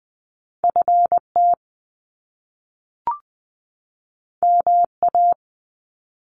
Morse code, heard as a keyed tone, about 20 wpm, 700 Hz. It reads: F T M A
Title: Top Two Letter Combinations in Words 20wpm